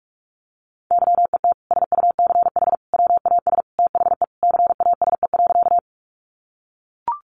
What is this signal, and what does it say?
Morse code, heard as a keyed tone, about 44 wpm, 700 Hz. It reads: YET SUCH WAS THE CASE.
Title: Sentences from Top 700 Words 45wpm